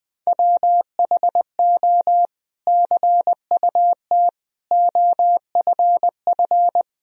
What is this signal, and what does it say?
Morse code, heard as a keyed tone, about 20 wpm, 700 Hz. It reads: WHO CUT OFF
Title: Sentences from Top 1000 Words - Review Part 1 - 20wpm